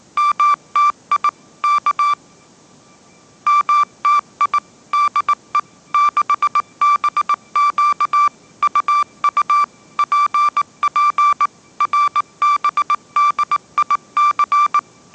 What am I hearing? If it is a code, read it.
Morse code, heard as a keyed tone, about 19 wpm, 1170 Hz. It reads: MTI K MTI DE 6BQ UU PP RBDIC
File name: mrx083 20wpm